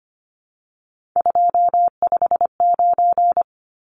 Morse code, heard as a keyed tone, about 25 wpm, 700 Hz. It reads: 259
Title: Numbers - 3-Digit 25wpm